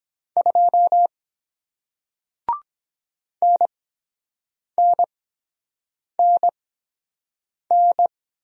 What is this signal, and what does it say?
Morse code, heard as a keyed tone, about 23 wpm, spaced as for 7 wpm, 700 Hz. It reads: 2 NNNN